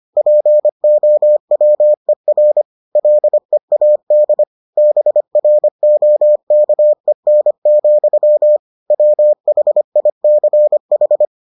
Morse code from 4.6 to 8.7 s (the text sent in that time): BROKEN,